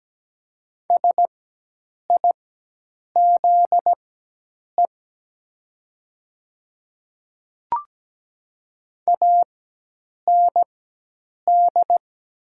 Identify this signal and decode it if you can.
Morse code, heard as a keyed tone, about 17 wpm, spaced as for 8 wpm, 700 Hz. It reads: SIZE AND
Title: 4x Character Spacing - Top 500 Words - 17wpm